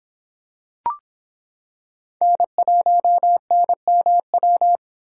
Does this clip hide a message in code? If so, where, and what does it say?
Yes: Morse code from 2.2 to 4.8 s, heard as a keyed tone, about 26 wpm, 700 Hz. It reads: N1NMW